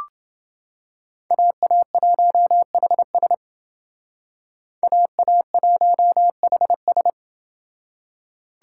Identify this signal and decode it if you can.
Morse code, heard as a keyed tone, about 28 wpm, 700 Hz. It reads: AA1HS AA1HS